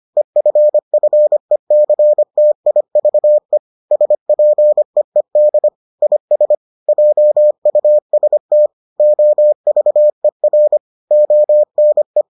Morse code, heard as a keyed tone, about 25 wpm, 600 Hz. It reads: EFFECTIVE SPEED IS JUST OVER ONE